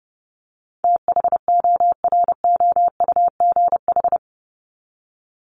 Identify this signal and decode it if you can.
Morse code, heard as a keyed tone, about 30 wpm, 700 Hz. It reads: THOROUGH
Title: LY Suffix - Top 100 Words with Base Words 30wpm